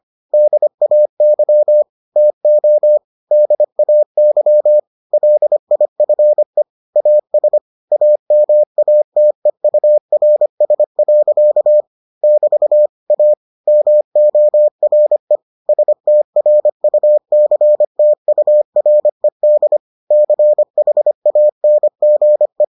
Morse code, heard as a keyed tone, about 25 wpm, 600 Hz. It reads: DAY TO DAY LIFE AS AMATEURS. <BT> A MORE STRUCTURED CHANGE